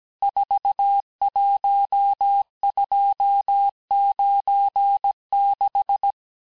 Morse code, heard as a keyed tone, about 17 wpm, 800 Hz. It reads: 41296
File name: corrige-cours-1581-texte-chiffre-vitesse-16-mpm